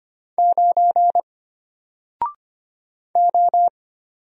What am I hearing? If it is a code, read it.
Morse code, heard as a keyed tone, about 25 wpm, 700 Hz. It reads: 9 O
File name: Single-Letter-Number-25wpm